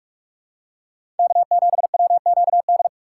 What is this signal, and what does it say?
Morse code, heard as a keyed tone, about 48 wpm, 700 Hz. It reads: K7WQD
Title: USA Call Signs - 5 Characters - 6x Character Spacing - 45wpm